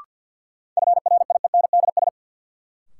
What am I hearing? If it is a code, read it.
Morse code, heard as a keyed tone, about 53 wpm, 700 Hz. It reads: FRIENDS
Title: Sentences from Top 1600 Words - New Word Drill Alt - 50wpm